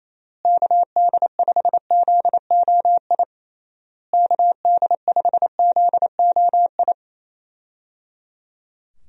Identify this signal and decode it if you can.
Morse code, heard as a keyed tone, about 27 wpm, 700 Hz. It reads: KD5ZOI KD5ZOI
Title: USA Call Signs - 6 Characters - Repeated 3x - 28wpm